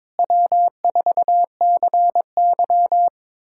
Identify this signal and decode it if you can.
Morse code, heard as a keyed tone, about 22 wpm, 700 Hz. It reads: W4CY
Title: USA Callsign DE Callsign - 4 Characters - 22wpm